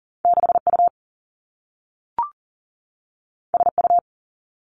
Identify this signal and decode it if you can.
Morse code, heard as a keyed tone, about 39 wpm, 700 Hz. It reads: 6U SU
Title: Two Challenging Letter-Number Combinations 40wpm